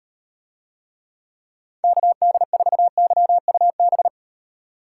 Morse code, heard as a keyed tone, about 38 wpm, 700 Hz. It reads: KD4YUB